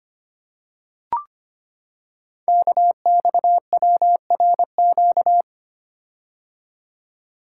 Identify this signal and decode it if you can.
Morse code, heard as a keyed tone, about 25 wpm, 700 Hz. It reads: KXWRQ
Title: Code Groups - 5 Letters - 25wpm